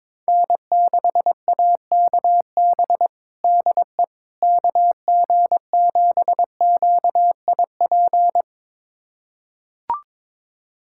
Morse code, heard as a keyed tone, about 22 wpm, 700 Hz. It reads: N6AKB DE KG7QIP
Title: USA Callsign DE Callsign 22wpm